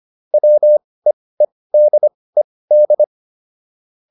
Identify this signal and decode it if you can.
Morse code, heard as a keyed tone, about 25 wpm, spaced as for 18 wpm, 600 Hz. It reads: WEEDED